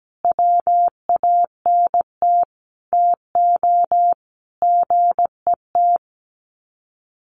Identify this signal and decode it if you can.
Morse code, heard as a keyed tone, about 17 wpm, 700 Hz. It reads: WANT TO GET
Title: Sets of 5 Words Encore 17wpm